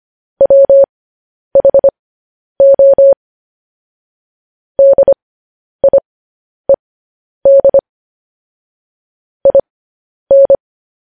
Morse code, heard as a keyed tone, about 25 wpm, spaced as for 10 wpm, 550 Hz. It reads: WHO DIED IN
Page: news-headlines-in-morse-code-at-10-wpm